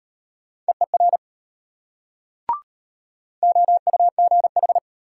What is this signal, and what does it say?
Morse code, heard as a keyed tone, about 34 wpm, 700 Hz. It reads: EER OUGH